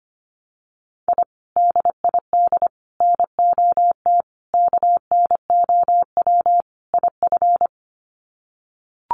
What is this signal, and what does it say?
Morse code, heard as a keyed tone, about 25 wpm, 700 Hz. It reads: I DID NOT KNOW IF
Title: Sets of 5 Words Encore 25wpm